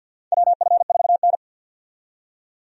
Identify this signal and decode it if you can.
Morse code, heard as a keyed tone, about 52 wpm, 700 Hz. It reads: WF4N
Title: USA Call Signs - 4 Characters - 50wpm